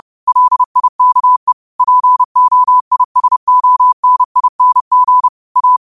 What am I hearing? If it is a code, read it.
Morse code, heard as a keyed tone, about 30 wpm, 1000 Hz. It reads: LIKE POISONING A
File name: qotdfiles / 30wpm